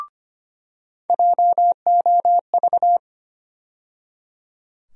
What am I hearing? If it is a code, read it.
Morse code, heard as a keyed tone, about 25 wpm, 700 Hz. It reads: JOV
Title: USA Call Sign Postfixes - 3 Letters 25wpm